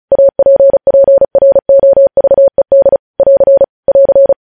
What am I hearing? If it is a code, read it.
Morse code, heard as a keyed tone, about 36 wpm, 550 Hz. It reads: APPROVED <AR> <AR>
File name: podcast / News headlines-1x-35x35